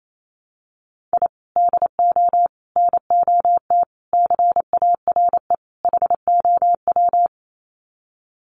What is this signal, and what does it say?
Morse code, heard as a keyed tone, about 28 wpm, 700 Hz. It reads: I DO NOT CARE HOW
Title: Sets of 5 Words Encore2 28wpm